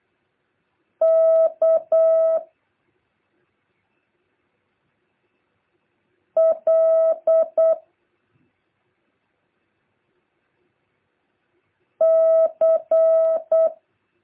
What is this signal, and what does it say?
Morse code, heard as a keyed tone, about 8 wpm, 650 Hz. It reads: K L C
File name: jh8cbh / 2023cw